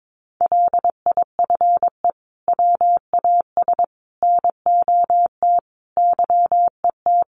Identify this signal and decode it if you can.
Morse code, heard as a keyed tone, about 22 wpm, 700 Hz. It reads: LIFE WAS NOT YET
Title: Sentences from Top 1200 Words 22wpm